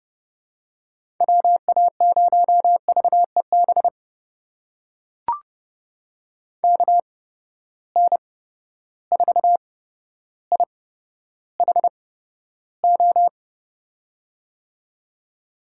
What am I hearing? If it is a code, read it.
Morse code, heard as a keyed tone, about 30 wpm, spaced as for 13 wpm, 700 Hz. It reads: WA0VEB KN4IHO
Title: USA Call Signs - 6 Characters - 8x Character Spacing - 30wpm